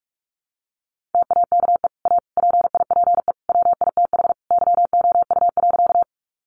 Morse code, heard as a keyed tone, about 44 wpm, 700 Hz. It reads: TAKE A PIPE WITH YOU.
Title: Sentences from Top 1500 Words 45wpm